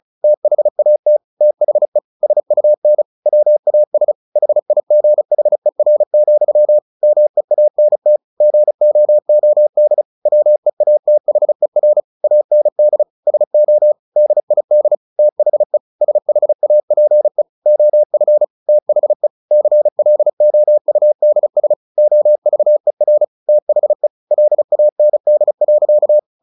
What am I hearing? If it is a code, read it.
Morse code, heard as a keyed tone, about 35 wpm, 600 Hz. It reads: THAT THE SUN WAS HIGHER, MEANT GOOD WEATHER AND SO DID THE SHAPE OF THE CLOUDS OVER THE LAND.
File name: the old man and the sea / the old man and the sea 1 35wpm